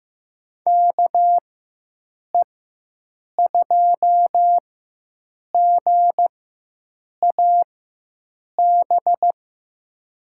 Morse code, heard as a keyed tone, about 15 wpm, spaced as for 7 wpm, 700 Hz. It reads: KE2GAB
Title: USA Call Signs - 4x Character Spacing - 15wpm